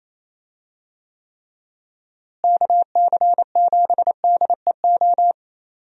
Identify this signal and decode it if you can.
Morse code, heard as a keyed tone, about 28 wpm, 700 Hz. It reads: KC7DEO
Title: USA Call Signs - 6 Characters - Repeated 2x - 28wpm